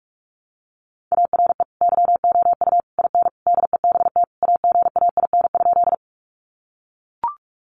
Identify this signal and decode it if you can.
Morse code, heard as a keyed tone, about 44 wpm, 700 Hz. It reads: ARE YOU IN DEBT AGAIN?
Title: Sentences from Top 1300 Words 45wpm